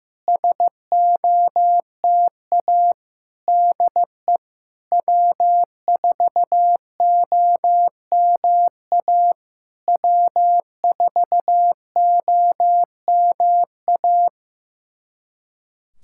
Morse code, heard as a keyed tone, about 15 wpm, 700 Hz. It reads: SOTA DE W4OMA W4OMA
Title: SOTA QSO Practice 15wpm